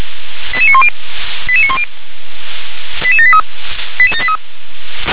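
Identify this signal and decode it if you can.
Five-tone selective call (ZVEI), heard as quick runs of five tones, tones of about 70 ms. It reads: 88910 88910 88783
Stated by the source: Received by SDR, ZVEI five-tone selective call.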